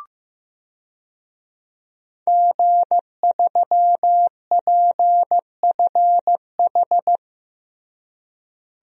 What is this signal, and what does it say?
Morse code, heard as a keyed tone, about 15 wpm, 700 Hz. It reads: G3PFH
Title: DXCC-100 Call Signs - 5 Characters - Repeated 3x - 15wpm